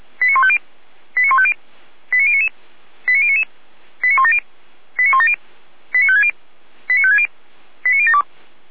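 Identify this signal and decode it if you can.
Five-tone selective call (ZVEI), heard as quick runs of five tones, tones of about 70 ms. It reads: 89150 89150 89099 89099 89170 89170 89560 89560 89072
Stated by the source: ZVEI five-tone selective call, address 89168, received on an SDR.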